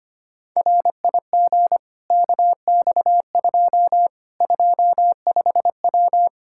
Morse code, heard as a keyed tone, about 25 wpm, 700 Hz. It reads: RIG KX2 25W
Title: K1USN SST QSO Practice 25wpm